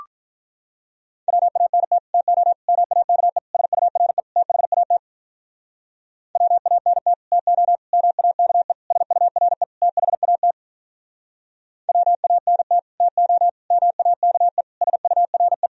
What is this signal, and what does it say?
Morse code, heard as a keyed tone, about 46 wpm, 700 Hz. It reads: WANT TO MAKE SURE THAT WANT TO MAKE SURE THAT WANT TO MAKE SURE